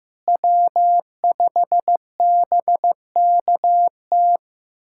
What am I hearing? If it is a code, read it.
Morse code, heard as a keyed tone, about 15 wpm, 700 Hz. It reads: W5BKT
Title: USA Callsign DE Callsign - 5 Characters - 15wpm